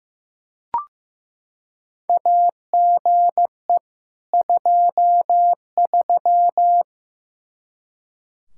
Morse code, heard as a keyed tone, about 16 wpm, 700 Hz. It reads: AGE 23
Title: Practice QSO - Exchange 3 - More Info - 15wpm